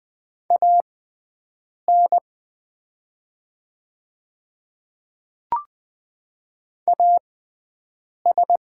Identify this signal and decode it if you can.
Morse code, heard as a keyed tone, about 20 wpm, spaced as for 7 wpm, 690 Hz. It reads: AN AS
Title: 6x Character Spacing - Top 100 Words - 20wpm